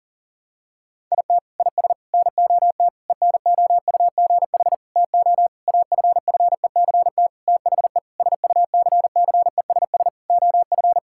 Morse code, heard as a keyed tone, about 40 wpm, 700 Hz. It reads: IT IS NOT ENOUGH TO AFFECT THE SUCCESS OF
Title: Sentences from Top 1000 Words - Review Part 6 - 40wpm